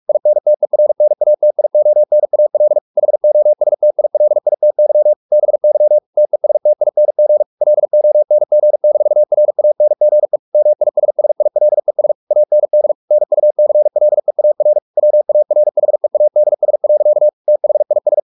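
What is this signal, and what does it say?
Morse code, heard as a keyed tone, about 44 wpm, 600 Hz. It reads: INTERNATIONAL HOSTILITY BY TESTING LONG-RANGE MISSILES AND NUCLEAR WARHEADS. THIS
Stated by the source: Morse practice at 45 wpm, the novel Never by Ken Follett in English, from CW med Gnister.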